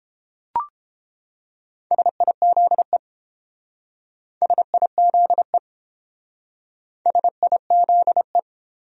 Morse code, heard as a keyed tone, about 30 wpm, 700 Hz. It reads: SIZE SIZE SIZE